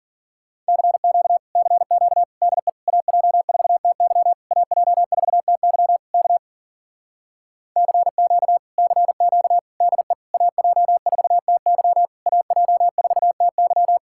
Morse code, heard as a keyed tone, about 52 wpm, 700 Hz. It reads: CQ CQ DE AJ4TY AJ4TY K CQ CQ DE AJ4TY AJ4TY